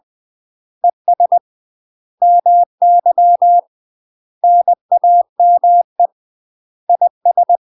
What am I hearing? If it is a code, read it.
Morse code, heard as a keyed tone, about 20 wpm, 700 Hz. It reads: ES MY NAME IS
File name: UK MASTER QSOs TRAINING / 20WPM / UK MASTER QSOs 0073